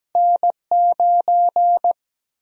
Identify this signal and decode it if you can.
Morse code, heard as a keyed tone, about 17 wpm, 700 Hz. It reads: N9
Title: USA Call Sign Prefixes - 17wpm